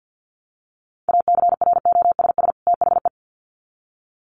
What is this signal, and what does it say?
Morse code, heard as a keyed tone, about 50 wpm, 700 Hz. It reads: ACROSS THE
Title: Sets of 2 Words 50wpm